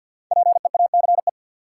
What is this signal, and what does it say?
Morse code, heard as a keyed tone, about 46 wpm, 700 Hz. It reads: PEACE
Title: Sentences from Top 1500 Words - New Word Drill Alt - 50wpm